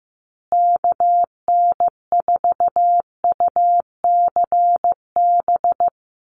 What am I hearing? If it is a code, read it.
Morse code, heard as a keyed tone, about 15 wpm, 700 Hz. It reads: KN4UCB
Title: USA Call Signs Repeated 2x 15wpm